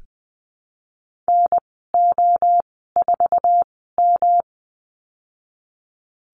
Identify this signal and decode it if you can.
Morse code, heard as a keyed tone, about 20 wpm, spaced as for 15 wpm, 700 Hz. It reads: NO4M
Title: 147 Lesson 24g - Letter y - Call Signs 20wpm